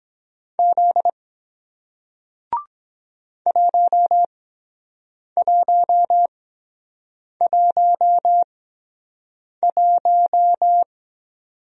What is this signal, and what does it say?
Morse code, heard as a keyed tone, about 23 wpm, spaced as for 7 wpm, 700 Hz. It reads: Z 1111